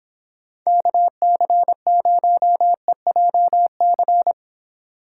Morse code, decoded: KC0EJC

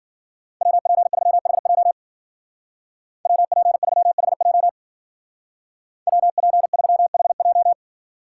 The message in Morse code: WP3HJ WP3HJ WP3HJ

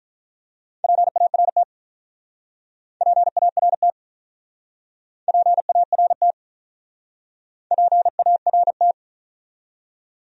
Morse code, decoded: PART PART PART PART

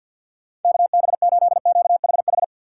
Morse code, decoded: KB8QHH

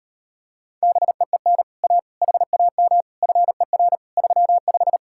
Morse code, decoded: BEEN A HAM FER 35

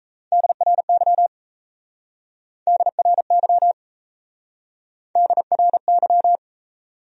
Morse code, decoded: DRY DRY DRY

